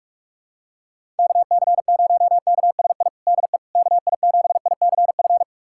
Morse code, heard as a keyed tone, about 42 wpm, 700 Hz. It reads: KC0KSI DE KI7ICF
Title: USA Callsign DE Callsign - 6 Characters - 45wpm